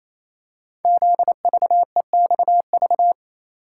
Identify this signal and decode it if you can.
Morse code, heard as a keyed tone, about 27 wpm, 700 Hz. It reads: ZVEXV
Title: Code Groups - 5 Letters - 28wpm